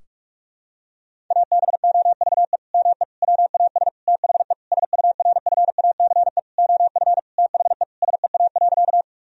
Morse code, heard as a keyed tone, about 46 wpm, 700 Hz. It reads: ABOVE ME WAS THE SURFACE OF THE SEA.
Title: Sentences from Top 1000 Words - Review Part 3 - 45wpm